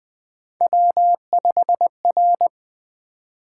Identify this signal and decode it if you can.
Morse code, heard as a keyed tone, about 20 wpm, 700 Hz. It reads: W5R
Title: Three Letter-Number Combinations - 20wpm